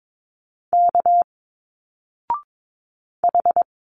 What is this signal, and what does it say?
Morse code, heard as a keyed tone, about 22 wpm, 700 Hz. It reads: K H
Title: Single Letters 22wpm